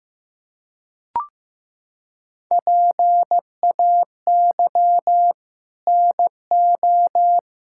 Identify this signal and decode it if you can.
Morse code, heard as a keyed tone, about 15 wpm, 700 Hz. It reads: PAY NO
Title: Sentences from Top 1000 Words - Review Part 6 - 15wpm